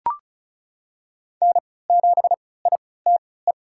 Morse code, heard as a keyed tone, about 34 wpm, spaced as for 20 wpm, 700 Hz. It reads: N7ITE
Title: USA Call Signs - 3x Character Spacing - 35wpm